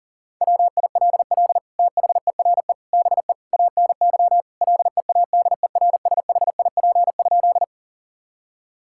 Morse code, heard as a keyed tone, about 40 wpm, 700 Hz. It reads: WILL THERE BE ANY LEADERSHIP?
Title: Sentences from Top 1600 Words 40wpm